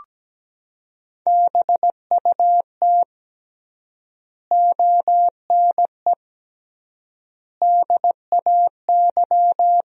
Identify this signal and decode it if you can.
Morse code, heard as a keyed tone, about 17 wpm, 700 Hz. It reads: BUT ONE DAY
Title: 3x Word-Spacing - Sentences from Top 400 Words - 17wpm